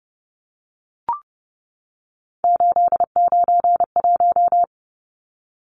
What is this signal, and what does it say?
Morse code, heard as a keyed tone, about 30 wpm, 700 Hz. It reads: 891
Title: Numbers - 3-Digit 30wpm